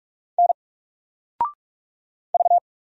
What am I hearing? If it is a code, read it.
Morse code, heard as a keyed tone, about 44 wpm, 700 Hz. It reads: N V
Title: Single Character (Letter:Number:Punctuation:Prosign) 45wpm